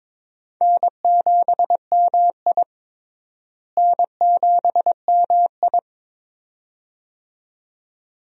Morse code, heard as a keyed tone, about 21 wpm, 700 Hz. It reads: N7MI N7MI
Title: USA Call Signs - 4 Characters - Repeated 2x - 22wpm